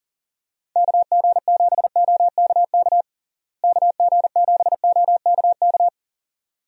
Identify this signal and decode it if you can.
Morse code, heard as a keyed tone, about 40 wpm, 700 Hz. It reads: KG7OKK KG7OKK
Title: USA Call Signs - 6 Characters - Repeated 2x - 40wpm